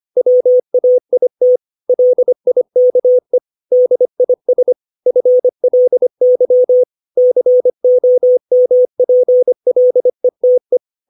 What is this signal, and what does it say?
Morse code, heard as a keyed tone, about 25 wpm, 500 Hz. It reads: WAIT LIKE DIS FLY COMPLETE
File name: 632 words / 632 Words 25WPM